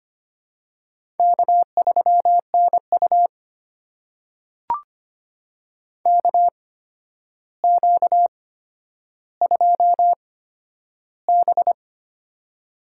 Morse code, heard as a keyed tone, about 25 wpm, spaced as for 11 wpm, 700 Hz. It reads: K3NU KQ2B
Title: USA Call Signs - 4 Characters - 8x Character Spacing - 25wpm